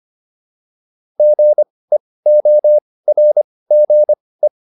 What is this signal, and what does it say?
Morse code, heard as a keyed tone, about 25 wpm, spaced as for 18 wpm, 600 Hz. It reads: GEORGE